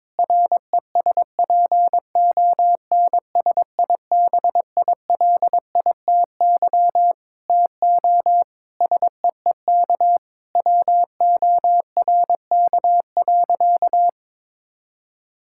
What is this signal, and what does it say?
Morse code, heard as a keyed tone, about 22 wpm, 700 Hz. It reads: RESPONSIBILITY TO SEEK WORK.
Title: Sentences from Top 1600 Words 22wpm